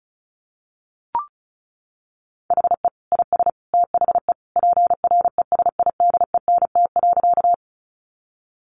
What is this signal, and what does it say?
Morse code, heard as a keyed tone, about 34 wpm, 700 Hz. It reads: HE IS THE PRESIDENT.
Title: Sentences from Top 1000 Words 35wpm